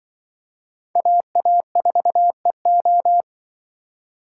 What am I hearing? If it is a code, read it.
Morse code, heard as a keyed tone, about 27 wpm, 700 Hz. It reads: AA4EO